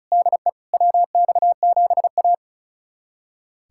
Morse code, heard as a keyed tone, about 36 wpm, 700 Hz. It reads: DE WX7A
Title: USA Callsign DE Callsign - 4 Characters - 35wpm